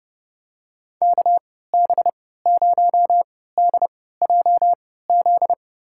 Morse code, heard as a keyed tone, about 30 wpm, spaced as for 17 wpm, 700 Hz. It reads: KB0DJZ
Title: USA Call Signs - 6 Characters - 3x Character Spacing - 30wpm